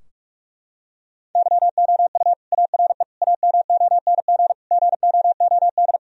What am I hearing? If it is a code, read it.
Morse code, heard as a keyed tone, about 44 wpm, 700 Hz. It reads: YOU ARE AMONG GOOD
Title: Sentences from Top 1000 Words - Review Part 3 - 45wpm